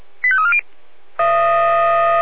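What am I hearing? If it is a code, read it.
Five-tone selective call (ZVEI), heard as quick runs of five tones, tones of about 70 ms. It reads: 85340
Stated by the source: ZVEI five-tone selective call received on an SDR.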